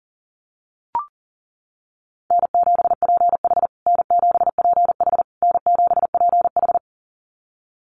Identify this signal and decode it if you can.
Morse code, heard as a keyed tone, about 40 wpm, 700 Hz. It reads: N7PH N7PH N7PH
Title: USA Call Signs Repeated 3x 40wpm